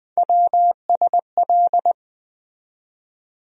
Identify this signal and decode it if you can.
Morse code, heard as a keyed tone, about 20 wpm, 700 Hz. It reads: WSL
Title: USA Call Sign Postfixes - 3 Letters 20wpm